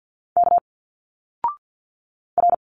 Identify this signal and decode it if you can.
Morse code, heard as a keyed tone, about 50 wpm, 700 Hz. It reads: K R
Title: Single Letters 50wpm